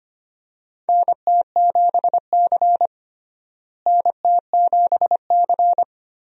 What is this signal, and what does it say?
Morse code, heard as a keyed tone, about 25 wpm, 700 Hz. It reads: NT7C NT7C
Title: USA Call Signs - 4 Characters - Repeated 3x - 25wpm